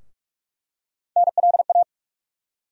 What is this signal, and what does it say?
Morse code, heard as a keyed tone, about 43 wpm, 700 Hz. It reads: NLA
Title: USA Call Sign Postfixes - 3 Letters 45wpm